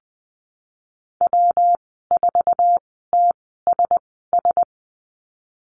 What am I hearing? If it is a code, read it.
Morse code, heard as a keyed tone, about 20 wpm, 700 Hz. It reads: W 4 T S S
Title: 154 Lesson 25g - Letter p - Call Signs 20wpm